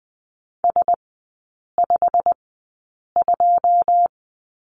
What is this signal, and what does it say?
Morse code, heard as a keyed tone, about 20 wpm, spaced as for 8 wpm, 690 Hz. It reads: S52
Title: 159 Lesson 26e - Letter g - Three Characters 20wpm